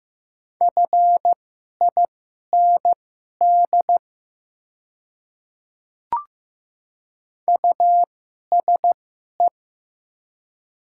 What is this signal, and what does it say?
Morse code, heard as a keyed tone, about 15 wpm, spaced as for 11 wpm, 700 Hz. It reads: FIND USE
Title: 2x Character Spacing - Top 200 Words - 15wpm